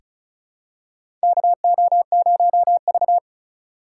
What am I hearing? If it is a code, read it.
Morse code, heard as a keyed tone, about 34 wpm, 700 Hz. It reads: KO0V
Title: USA Call Signs - 3x Character Spacing - 35wpm